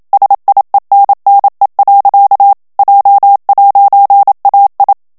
Morse code, heard as a keyed tone, about 27 wpm, 800 Hz. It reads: SIENNE. J'AI
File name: FRANCAIS TEXTE CLAIR 2 VITESSE 26 MOTS MINUTES